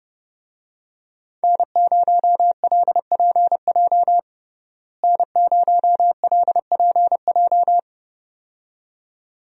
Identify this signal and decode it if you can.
Morse code, heard as a keyed tone, about 32 wpm, 700 Hz. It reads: N0LPJ N0LPJ
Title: USA Call Signs - 5 Characters - Repeated 2x - 30wpm